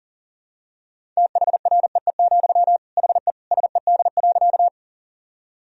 Morse code, heard as a keyed tone, about 40 wpm, 700 Hz. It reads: THREE, HE SED.
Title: Sentences from Top 200 Words and Abbreviations 40wpm